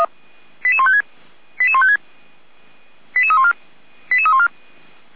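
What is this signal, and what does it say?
Five-tone selective call (ZVEI), heard as quick runs of five tones, tones of about 70 ms. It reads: 88156 88156 88314 88314
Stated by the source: ZVEI five-tone selective call 88402 received on SDR, from the audio server of Feuerwehr Orlamünde (fire brigade).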